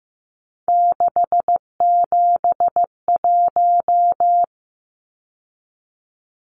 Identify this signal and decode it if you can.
Morse code, heard as a keyed tone, about 15 wpm, 700 Hz. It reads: 671
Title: Numbers - 3-Digit 15wpm